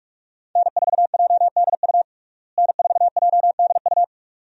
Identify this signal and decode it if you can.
Morse code, heard as a keyed tone, about 47 wpm, 700 Hz. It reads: N4JDU N4JDU
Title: USA Call Signs - 5 Characters - Repeated 2x - 45wpm